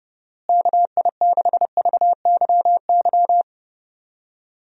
Morse code, heard as a keyed tone, about 32 wpm, 700 Hz. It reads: KI6VYY
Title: USA Call Signs - 6 Characters - Repeated 3x - 30wpm